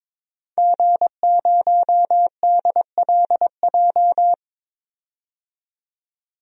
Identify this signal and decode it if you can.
Morse code, heard as a keyed tone, about 22 wpm, 700 Hz. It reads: G0DLJ
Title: DXCC-100 Call Signs - 5 Characters - 22wpm